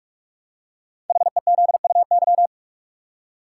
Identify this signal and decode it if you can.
Morse code, heard as a keyed tone, about 44 wpm, 700 Hz. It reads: HEZUY
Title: Code Groups - 5 Letters - 45wpm